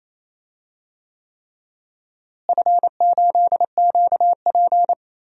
Morse code, heard as a keyed tone, about 28 wpm, 700 Hz. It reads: F8QP
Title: DXCC-100 Call Signs - 4 Characters - Repeated 3x - 28wpm